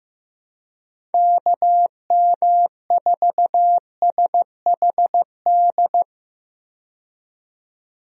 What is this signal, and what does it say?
Morse code, heard as a keyed tone, about 16 wpm, 700 Hz. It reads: KM4SHD
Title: USA Call Signs - 6 Characters - 15wpm